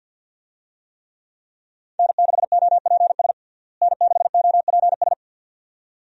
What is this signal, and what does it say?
Morse code, heard as a keyed tone, about 51 wpm, 700 Hz. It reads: N6OPS N6OPS
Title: USA Call Signs - 5 Characters - Repeated 2x - 50wpm